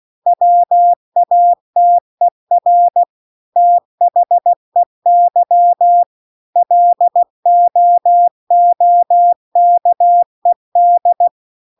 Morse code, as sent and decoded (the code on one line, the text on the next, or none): .-- .- - . .-. / - .... -.-- / .-.. --- --- -.- . -..
WATER THEY LOOKED